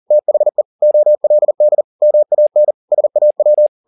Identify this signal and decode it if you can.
Morse code, heard as a keyed tone, about 40 wpm, 600 Hz. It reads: THE OLD MAN SAW